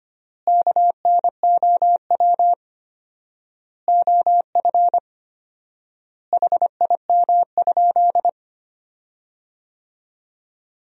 Morse code, heard as a keyed tone, about 25 wpm, 700 Hz. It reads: KNOW OF HIM?
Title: 4x Word-Spacing - Sentences from Top 100 Words - 25wpm